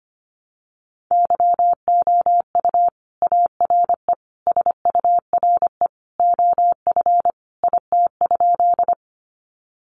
Morse code, heard as a keyed tone, about 25 wpm, 700 Hz. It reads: YOU ARE SURE OF IT?
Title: Sentences from Top 300 Words 25wpm